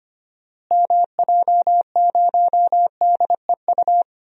Morse code, heard as a keyed tone, about 25 wpm, 700 Hz. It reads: MJ0DEU
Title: DXCC-100 Call Signs - 6 Characters - Repeated 2x - 25wpm